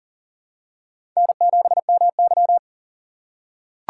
Morse code, heard as a keyed tone, about 39 wpm, 700 Hz. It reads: N7MY
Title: USA Call Signs - 4 Characters - 4x Character Spacing - 40wpm